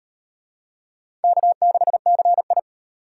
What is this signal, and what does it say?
Morse code, heard as a keyed tone, about 37 wpm, 700 Hz. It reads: K6CI